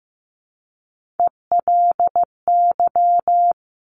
Morse code, heard as a keyed tone, about 15 wpm, 700 Hz. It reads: ELY
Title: Top Three Letter Combinations in Words 15wpm